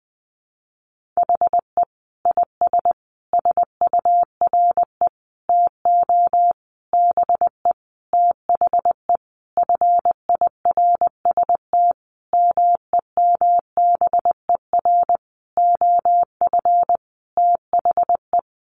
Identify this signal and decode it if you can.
Morse code, heard as a keyed tone, about 20 wpm, 700 Hz. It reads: HE IS SURE TO BE THE FIRST MEMBER OF THE
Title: Sentences from Top 1400 Words 20wpm